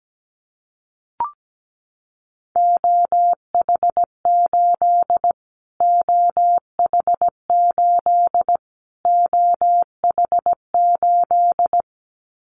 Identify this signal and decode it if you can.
Morse code, heard as a keyed tone, about 17 wpm, 700 Hz. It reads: OH8 OH8 OH8
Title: Callsign Prefixes for Top 25 DXCC - Repeated 3x - 17wpm